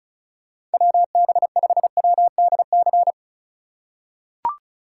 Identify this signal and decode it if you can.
Morse code, heard as a keyed tone, about 36 wpm, 700 Hz. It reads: WB5WDC
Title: USA Call Signs - 6 Characters - 4x Character Spacing - 35wpm